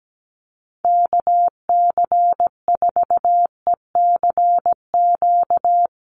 Morse code, heard as a keyed tone, about 17 wpm, 700 Hz. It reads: KC4ECQ
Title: USA Call Signs 17wpm